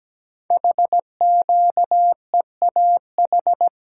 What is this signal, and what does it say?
Morse code, heard as a keyed tone, about 17 wpm, 700 Hz. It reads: HQEAH
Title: Code Groups - 5 Letters - 17wpm